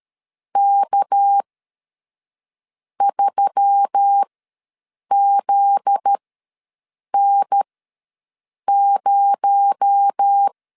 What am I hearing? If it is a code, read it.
Morse code, heard as a keyed tone, about 13 wpm, spaced as for 6 wpm, 800 Hz. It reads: K3ZN0
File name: 7 GF9Z ws 12wpm